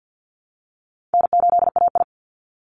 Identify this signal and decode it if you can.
Morse code, heard as a keyed tone, about 50 wpm, 700 Hz. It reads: N8AI
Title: Easy Call Signs - 50wpm